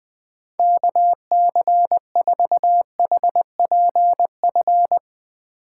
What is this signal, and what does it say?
Morse code, heard as a keyed tone, about 20 wpm, 700 Hz. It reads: KC4HPF
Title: Speed Racing - USA Call Signs 20wpm